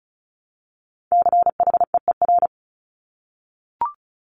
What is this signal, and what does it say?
Morse code, heard as a keyed tone, about 36 wpm, 700 Hz. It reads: CHEER
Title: Multi-letter Phonemes Repeated 3x 35wpm